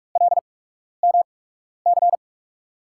Morse code, heard as a keyed tone, about 43 wpm, spaced as for 12 wpm, 700 Hz. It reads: LMC
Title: USA Call Signs - 6 Characters - 8x Character Spacing - 45wpm